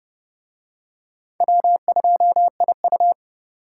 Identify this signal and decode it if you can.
Morse code, heard as a keyed tone, about 28 wpm, 700 Hz. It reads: W2IU